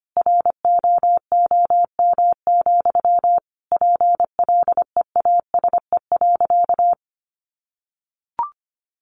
Morse code, heard as a keyed tone, about 25 wpm, 700 Hz. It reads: ROOM, PLEASE.
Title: Sentences from Top 1500 Words 25wpm